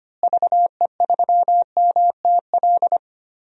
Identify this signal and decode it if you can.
Morse code, heard as a keyed tone, about 25 wpm, 700 Hz. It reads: VE3MTL
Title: DXCC-100 Call Signs - 6 Characters - Repeated 2x - 25wpm